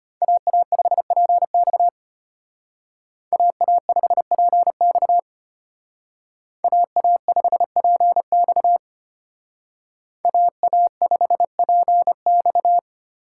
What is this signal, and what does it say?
Morse code, heard as a keyed tone, about 32 wpm, 700 Hz. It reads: AA5PX AA5PX AA5PX AA5PX